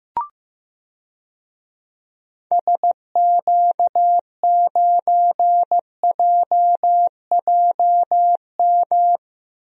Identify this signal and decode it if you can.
Morse code, heard as a keyed tone, about 15 wpm, 700 Hz. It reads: SQ9JJM
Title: DXCC-100 Call Signs - 6 Characters - 15wpm